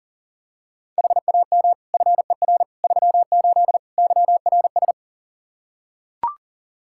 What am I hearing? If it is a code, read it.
Morse code, heard as a keyed tone, about 40 wpm, 700 Hz. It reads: HAM FER 38 YRS
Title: K1USN SST QSO Practice 40wpm